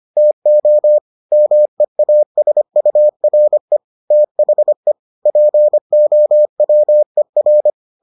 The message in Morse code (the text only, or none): TO MEASURE THE POWER